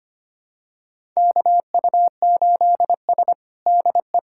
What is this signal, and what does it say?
Morse code, heard as a keyed tone, about 25 wpm, 700 Hz. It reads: KU8S DE
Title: USA Callsign DE Callsign 25wpm